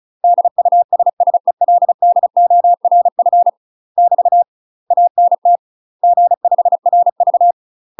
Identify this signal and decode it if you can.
Morse code, heard as a keyed tone, about 34 wpm, 700 Hz. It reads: DUSSELDORF <BT> ANT G5RV
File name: UK MASTER QSOs TRAINING / 35WPM / UK MASTER QSOs 0015